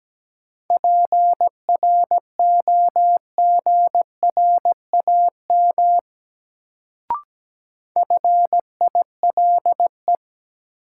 Morse code, heard as a keyed tone, about 17 wpm, 700 Hz. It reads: PROGRAM FILE
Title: Sentences from Top 1000 Words - Review Part 4 - New Word Drill - 17wpm